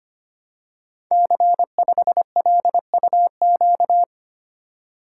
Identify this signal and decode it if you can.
Morse code, heard as a keyed tone, about 25 wpm, 700 Hz. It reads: C5LUQ
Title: Code Groups - 5 Letter:Numbers - 25wpm